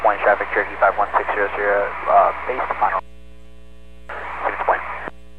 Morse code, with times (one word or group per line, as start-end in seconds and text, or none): none